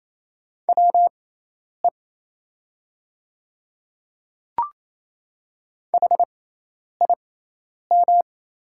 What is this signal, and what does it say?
Morse code, heard as a keyed tone, about 28 wpm, spaced as for 10 wpm, 700 Hz. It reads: WE HIM
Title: 6x Character Spacing - Top 100 Words - 28wpm